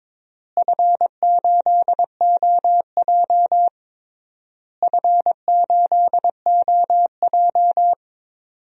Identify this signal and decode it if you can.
Morse code, heard as a keyed tone, about 22 wpm, 700 Hz. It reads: F8OJ F8OJ
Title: DXCC-100 Call Signs - 4 Characters - Repeated 3x - 22wpm